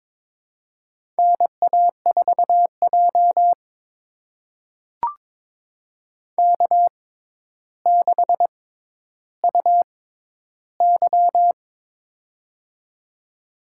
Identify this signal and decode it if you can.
Morse code, heard as a keyed tone, about 23 wpm, spaced as for 11 wpm, 700 Hz. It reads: NA4J K6UY
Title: USA Call Signs - 4 Characters - 6x Character Spacing - 22wpm